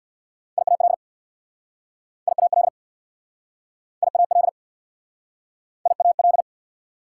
Morse code, heard as a keyed tone, about 67 wpm, 690 Hz. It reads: IAL IAL IAL IAL